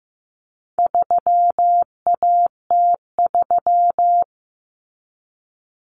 Morse code, heard as a keyed tone, about 15 wpm, 700 Hz. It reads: 3AT3
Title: Common Cut Numbers - 4-Digit 15wpm